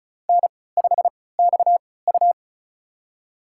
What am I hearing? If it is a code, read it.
Morse code, heard as a keyed tone, about 36 wpm, spaced as for 20 wpm, 700 Hz. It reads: N5XU
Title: USA Call Signs - 4 Characters - 3x Character Spacing - 35wpm